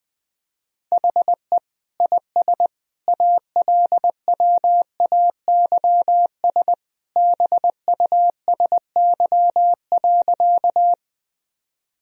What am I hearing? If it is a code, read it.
Morse code, heard as a keyed tone, about 20 wpm, 700 Hz. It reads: HE IS ALWAYS BUSY.
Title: Sentences from Top 1000 Words - Review Part 3 - 20wpm